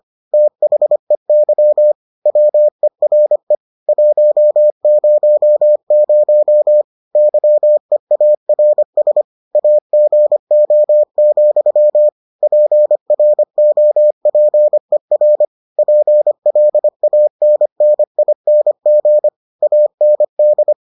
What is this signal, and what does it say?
Morse code, heard as a keyed tone, about 25 wpm, 600 Hz. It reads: THEY WERE 100 YEARS AGO, PROPER PLANNING AND